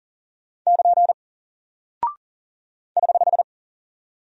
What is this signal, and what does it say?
Morse code, heard as a keyed tone, about 38 wpm, 700 Hz. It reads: <KN> <HH>
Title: Single Character (Letter:Number:Punctuation:Prosign) 40wpm